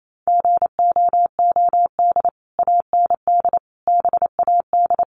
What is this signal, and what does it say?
Morse code, heard as a keyed tone, about 28 wpm, 700 Hz. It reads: GOOD AND BAD
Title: Sentences from Top 1500 Words 28wpm